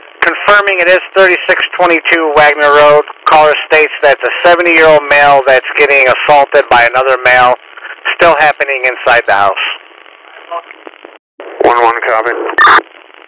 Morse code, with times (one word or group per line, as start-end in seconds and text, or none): none